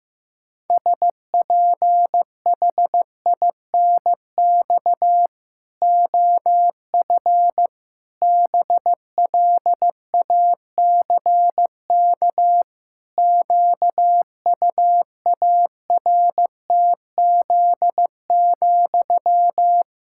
Morse code, heard as a keyed tone, about 16 wpm, 700 Hz. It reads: SPHINX OF BLACK QUARTZ,
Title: Pangram Challenge v2 - 15wpm